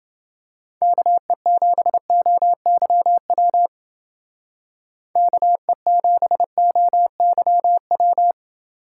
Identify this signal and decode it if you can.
Morse code, heard as a keyed tone, about 29 wpm, 700 Hz. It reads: KE7OYW KE7OYW